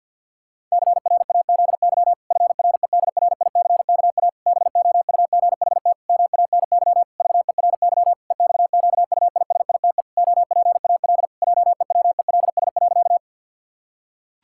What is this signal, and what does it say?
Morse code, heard as a keyed tone, about 50 wpm, 700 Hz. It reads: CRAZY FREDRICKA BOUGHT MANY VERY EXQUISITE OPAL JEWELS.
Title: Pangram Challenge v2 - 50wpm